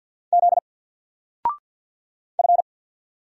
Morse code, heard as a keyed tone, about 50 wpm, 700 Hz. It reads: Z F
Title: Single Letter-Number 50wpm